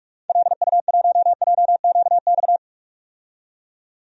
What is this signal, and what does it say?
Morse code, heard as a keyed tone, about 46 wpm, 700 Hz. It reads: LU1JQX
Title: DXCC-100 Call Signs - 6 Characters - 45wpm